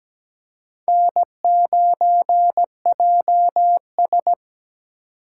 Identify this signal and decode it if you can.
Morse code, heard as a keyed tone, about 17 wpm, 700 Hz. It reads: N9JS
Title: USA Call Signs - 4 Characters - 8x Character Spacing - 17wpm